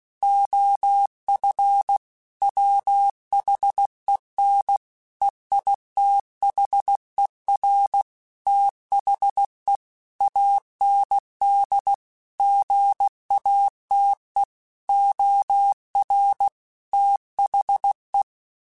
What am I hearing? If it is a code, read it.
Morse code, heard as a keyed tone, about 16 wpm, 800 Hz. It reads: OF WHEN EITHER THE AND GATE OR THE
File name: Technical Subjects / Technical Subjects-mp3s / AND Gate @15